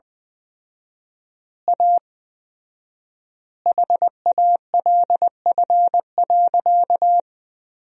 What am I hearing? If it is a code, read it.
Morse code, heard as a keyed tone, about 20 wpm, 700 Hz. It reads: A HALF.
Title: 4x Word-Spacing - Sentences from Top 400 Words - 20wpm